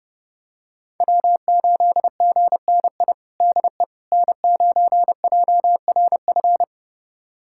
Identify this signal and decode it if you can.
Morse code, heard as a keyed tone, about 30 wpm, 700 Hz. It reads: W8GNI DE N9JRF
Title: USA Callsign DE Callsign - 5 Characters - 30wpm